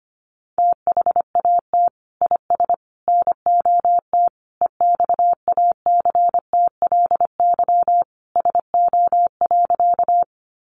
Morse code, heard as a keyed tone, about 25 wpm, 700 Hz. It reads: THAT IS NOT EXACTLY SO.
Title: Sentences from Top 900 Words 25wpm